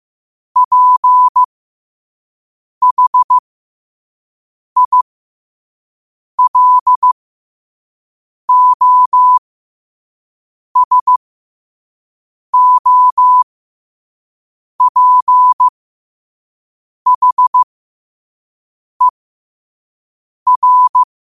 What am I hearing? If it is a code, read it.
Morse code, heard as a keyed tone, about 15 wpm, spaced as for 5 wpm, 1000 Hz. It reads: PHILOSOPHER
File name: qotdfiles / 5wpm